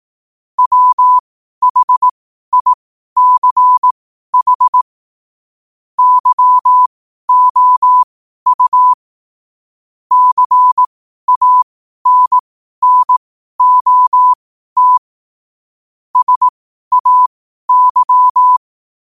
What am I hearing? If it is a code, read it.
Morse code, heard as a keyed tone, about 18 wpm, spaced as for 12 wpm, 1000 Hz. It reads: WHICH YOU CANNOT SAY
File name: qotdfiles / 12wpm